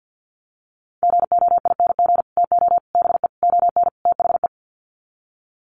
Morse code, heard as a keyed tone, about 50 wpm, 700 Hz. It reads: GOING TO BE ON THE
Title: Sets of 5 Words Encore2 50wpm